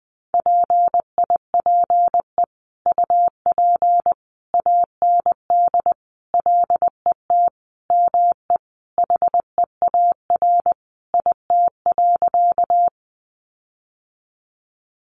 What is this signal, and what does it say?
Morse code, heard as a keyed tone, about 20 wpm, 700 Hz. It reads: PIPE UP AND LET ME HEAR IT.
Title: Sentences from Top 1500 Words 20wpm